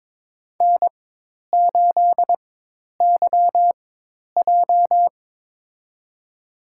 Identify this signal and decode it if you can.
Morse code, heard as a keyed tone, about 22 wpm, spaced as for 10 wpm, 700 Hz. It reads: N8YJ
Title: USA Call Signs - 4 Characters - 4x Character Spacing - 22wpm